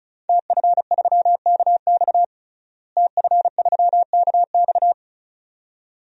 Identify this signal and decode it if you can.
Morse code, heard as a keyed tone, about 34 wpm, 700 Hz. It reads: TF3KX TF3KX
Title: DXCC-100 Call Signs - 5 Characters - Repeated 2x - 35wpm